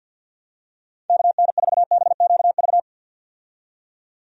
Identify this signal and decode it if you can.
Morse code, heard as a keyed tone, about 52 wpm, 700 Hz. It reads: KN4BQV
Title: USA Call Signs - 6 Characters - 50wpm